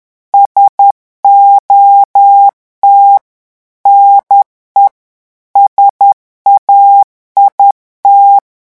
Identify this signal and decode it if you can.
Morse code, heard as a keyed tone, about 10 wpm, 800 Hz. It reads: SOT NE SAIT
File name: morse mp3 / revision francais 10 mots minute